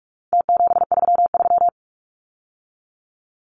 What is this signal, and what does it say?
Morse code, heard as a keyed tone, about 44 wpm, 700 Hz. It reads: T733
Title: Numbers - Contest Style - 4-Digit 45wpm